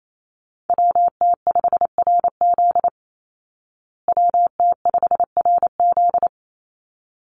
Morse code, heard as a keyed tone, about 28 wpm, 700 Hz. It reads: WT5RZ WT5RZ
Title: Call Signs Repeated 3x - 28wpm